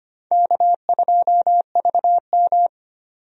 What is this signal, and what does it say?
Morse code, heard as a keyed tone, about 25 wpm, 700 Hz. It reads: K2VM
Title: USA Call Signs - 4 Characters - 25wpm